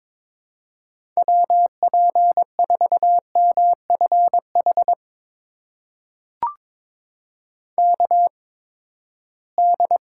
Morse code, heard as a keyed tone, about 23 wpm, spaced as for 15 wpm, 700 Hz. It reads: WP4MFH KD
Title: USA Call Signs - 8x Character Spacing - 22wpm